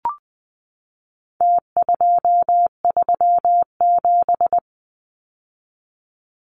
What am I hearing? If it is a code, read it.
Morse code, heard as a keyed tone, about 20 wpm, 700 Hz. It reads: T237
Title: Numbers - Contest Style - 4-Digit 20wpm